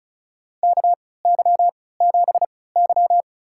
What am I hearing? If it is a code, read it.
Morse code, heard as a keyed tone, about 36 wpm, spaced as for 20 wpm, 700 Hz. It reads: KY7Y
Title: USA Call Signs - 3x Character Spacing - 35wpm